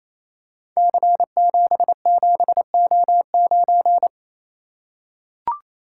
Morse code, heard as a keyed tone, about 29 wpm, 700 Hz. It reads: C77O9
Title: Code Groups - 5 Letter:Numbers - 28wpm